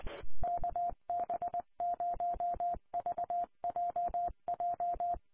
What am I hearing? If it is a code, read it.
Morse code, heard as a keyed tone, about 20 wpm, 700 Hz. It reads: KB0VJJ
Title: Colorado Connection Statewide Linked Repeaters 447.525